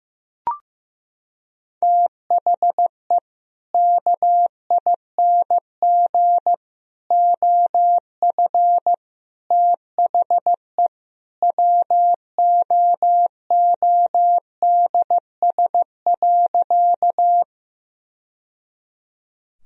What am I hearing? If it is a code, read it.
Morse code, heard as a keyed tone, about 15 wpm, 700 Hz. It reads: THE KING OF THE WOODS.
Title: Sentences from Top 1000 Words - Review Part 2 - 15wpm